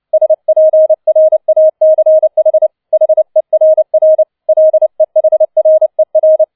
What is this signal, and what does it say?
Morse code, heard as a keyed tone, about 29 wpm, 630 Hz. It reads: SPRACH HERR LEHRER
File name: 007 Letzter Streich 30wpm